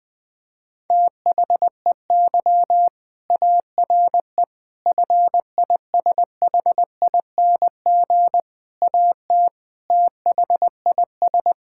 Morse code, heard as a keyed tone, about 20 wpm, 700 Hz. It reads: THEY ARE FISHING AT THIS
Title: Sentences from Top 1000 Words - Review Part 5 - 20wpm